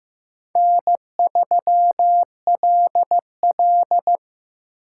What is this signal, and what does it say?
Morse code, heard as a keyed tone, about 17 wpm, 700 Hz. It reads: N3LL
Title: Speed Racing - USA 1x2 Call Signs 15wpm